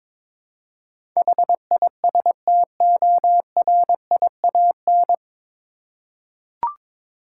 Morse code, heard as a keyed tone, about 22 wpm, 700 Hz. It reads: HISTORIAN
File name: 9 Letter Words - First 3 Letters Given - 22wpm